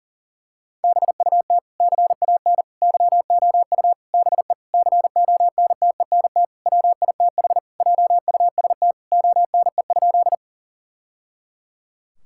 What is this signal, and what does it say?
Morse code, heard as a keyed tone, about 37 wpm, 700 Hz. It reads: BUT CAN YOU BE CONTENT WITH JUST ONE?
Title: Sentences from Top 1000 Words - Review Part 6 - 40wpm